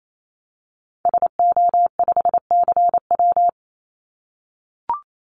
Morse code, heard as a keyed tone, about 28 wpm, 700 Hz. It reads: SO5CW
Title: Call Signs - 28wpm